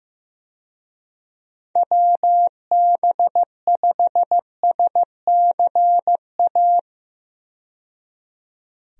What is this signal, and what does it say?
Morse code, heard as a keyed tone, about 15 wpm, 700 Hz. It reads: WB5SCA
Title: USA Call Signs - 6 Characters - 15wpm